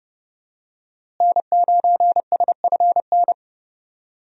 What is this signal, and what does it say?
Morse code, heard as a keyed tone, about 30 wpm, 700 Hz. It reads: N9SFN
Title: USA Call Signs - 5 Characters - Repeated 3x - 30wpm